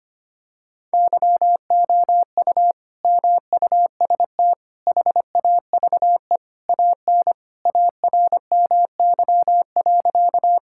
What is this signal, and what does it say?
Morse code, heard as a keyed tone, about 25 wpm, 700 Hz. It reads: YOU MUST HAVE AN ARMY.
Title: Sentences from Top 1000 Words - Review Part 5 - 25wpm